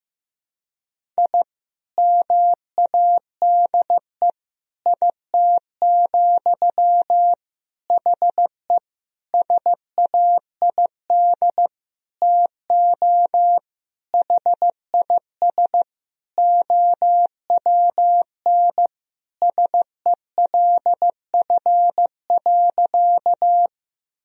Morse code, heard as a keyed tone, about 15 wpm, 700 Hz. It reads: I MADE IT, HE SAID TO HIS OWN SELF.
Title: Sentences from Top 1000 Words - Review Part 1 - 15wpm